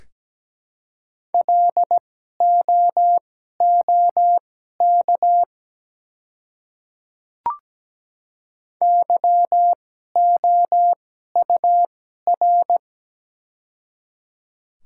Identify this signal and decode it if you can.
Morse code, heard as a keyed tone, about 17 wpm, spaced as for 12 wpm, 700 Hz. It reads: LOOK YOUR
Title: 2x Character Spacing - Top 100 Words - 17wpm